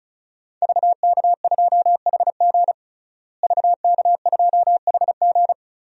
Morse code, heard as a keyed tone, about 34 wpm, 700 Hz. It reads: VK2HG VK2HG
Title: DXCC-100 Call Signs - 5 Characters - Repeated 3x - 35wpm